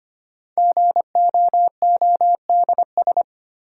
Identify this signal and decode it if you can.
Morse code, heard as a keyed tone, about 25 wpm, 700 Hz. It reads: GOODS
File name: 5 Letter Words - First 3 Letters Given - 25wpm